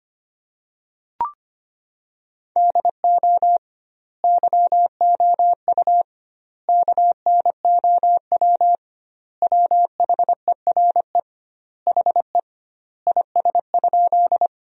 Morse code, heard as a keyed tone, about 25 wpm, 700 Hz. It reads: DO YOU KNOW WHERE HE IS?
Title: 2x Word-Spacing - Sentences from Top 300 Words - 25wpm